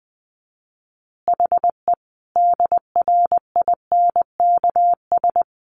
Morse code, heard as a keyed tone, about 20 wpm, 700 Hz. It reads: HE DRINKS
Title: Easy Phrases 20wpm